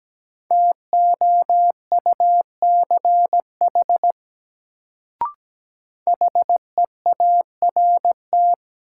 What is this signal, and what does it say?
Morse code, heard as a keyed tone, about 17 wpm, 700 Hz. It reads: TOUCH HEART
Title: Sentences from Top 1200 Words - New Word Drill - 17wpm